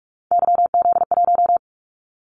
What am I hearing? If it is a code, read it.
Morse code, heard as a keyed tone, about 46 wpm, 700 Hz. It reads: YZ1
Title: Callsign Prefixes for Top 25 DXCC - 45wpm